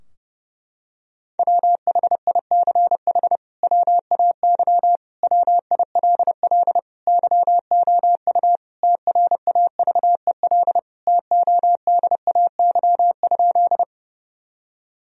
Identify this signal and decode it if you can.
Morse code, heard as a keyed tone, about 31 wpm, 700 Hz. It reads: WHICH WAY WILL YOU TRAVEL TODAY?
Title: Sentences from Top 1000 Words - Review Part 2 - 30wpm